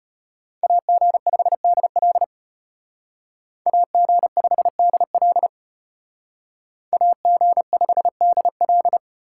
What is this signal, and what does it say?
Morse code, heard as a keyed tone, about 33 wpm, 700 Hz. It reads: AG5DL AG5DL AG5DL